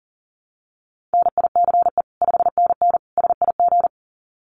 Morse code, heard as a keyed tone, about 40 wpm, 700 Hz. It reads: NICE 5NN SIG
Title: Common QSO Phrases 40wpm